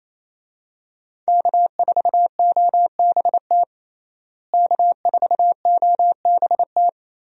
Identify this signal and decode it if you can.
Morse code, heard as a keyed tone, about 28 wpm, 700 Hz. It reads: K4OBT K4OBT
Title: USA Call Signs - 5 Characters - Repeated 3x - 28wpm